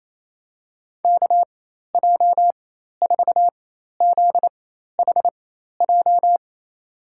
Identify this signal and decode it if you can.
Morse code, heard as a keyed tone, about 28 wpm, spaced as for 13 wpm, 700 Hz. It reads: KJ4ZHJ
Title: USA Call Signs - 6 Characters - 4x Character Spacing - 28wpm